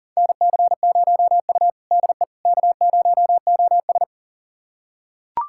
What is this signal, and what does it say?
Morse code, heard as a keyed tone, about 37 wpm, 700 Hz. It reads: NC0U DE K0OS
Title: USA Callsign DE Callsign - 4 Characters - 40wpm